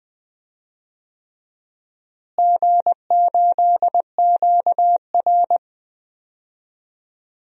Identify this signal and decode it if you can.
Morse code, heard as a keyed tone, about 21 wpm, 700 Hz. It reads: G8QR
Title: DXCC-100 Call Signs - 4 Characters - Repeated 3x - 20wpm